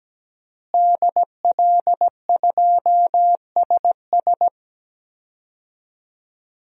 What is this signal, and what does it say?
Morse code, heard as a keyed tone, about 17 wpm, 700 Hz. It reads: DL2SS
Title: DXCC-100 Call Signs - 5 Characters - Repeated 3x - 17wpm